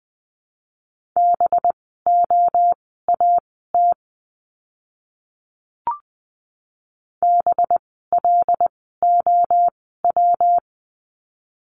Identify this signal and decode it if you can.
Morse code, heard as a keyed tone, about 20 wpm, spaced as for 14 wpm, 700 Hz. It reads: BOAT BLOW
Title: 185 Lesson 30f - Letter b - Words 20wpm